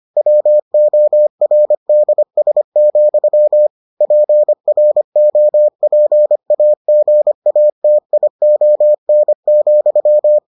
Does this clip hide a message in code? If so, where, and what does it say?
Yes: Morse code from 0.2 to 10.4 s, heard as a keyed tone, about 25 wpm, 600 Hz. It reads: WORDS, PROPAGATION,